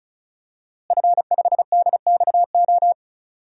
Morse code, heard as a keyed tone, about 36 wpm, 700 Hz. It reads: F5DXO
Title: DXCC-100 Call Signs - 5 Characters - Repeated 3x - 35wpm